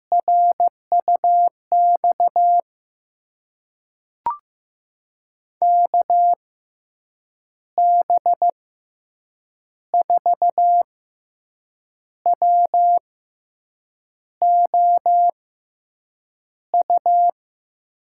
Morse code, heard as a keyed tone, about 15 wpm, spaced as for 6 wpm, 700 Hz. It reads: RUX KB4WOU
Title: USA Call Signs - 6 Characters - 6x Character Spacing - 15wpm